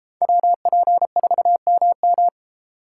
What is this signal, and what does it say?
Morse code, heard as a keyed tone, about 33 wpm, 700 Hz. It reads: WP4MM